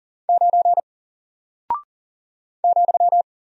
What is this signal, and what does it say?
Morse code, heard as a keyed tone, about 40 wpm, 700 Hz. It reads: 9 ,
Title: Single Character (Letter:Number:Punctuation:Prosign) Alt 40wpm